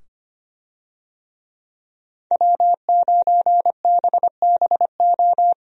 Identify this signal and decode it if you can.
Morse code, heard as a keyed tone, about 25 wpm, 700 Hz. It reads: W9BBO
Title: USA Call Signs - 5 Characters - 25wpm